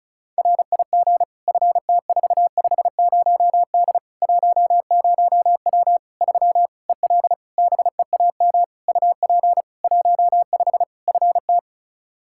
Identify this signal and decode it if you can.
Morse code, heard as a keyed tone, about 36 wpm, 700 Hz. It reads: RIG FT450D 10W 3 EL BEAM UP 15 FT
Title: Practice QSO - Exchange 2 - Info - 35wpm